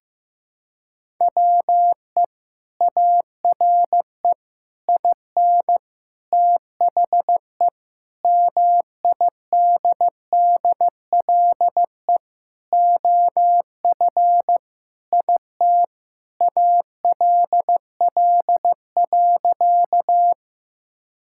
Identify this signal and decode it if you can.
Morse code, heard as a keyed tone, about 15 wpm, 700 Hz. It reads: WE ARE IN THE MIDDLE OF IT ALL.
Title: Sentences from Top 1000 Words - Review Part 6 - 15wpm